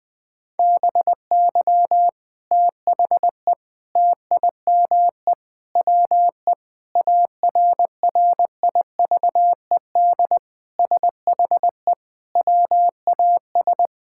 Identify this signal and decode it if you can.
Morse code, heard as a keyed tone, about 20 wpm, 700 Hz. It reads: BY THE TIME WE ARRIVED SHE WAS
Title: Sentences with 3-Word Phrases - 20wpm